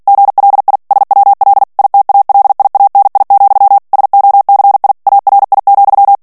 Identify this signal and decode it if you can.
Morse code, heard as a keyed tone, about 48 wpm, 800 Hz. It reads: GLI SWL ITALIANI, SOCI ARI,
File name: ITALIEN texte clair 2 vitesse 45 mots minute